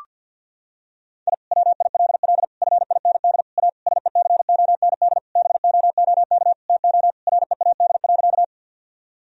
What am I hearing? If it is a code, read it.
Morse code, heard as a keyed tone, about 46 wpm, 700 Hz. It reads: I WILL FIND A SECOND BOOK TO READ.
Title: Sentences from Top 1000 Words - Review Part 1 - 50wpm